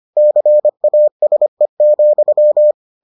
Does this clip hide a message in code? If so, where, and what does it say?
Yes: Morse code from 0.2 to 2.7 s, heard as a keyed tone, about 25 wpm, 600 Hz. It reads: CASE,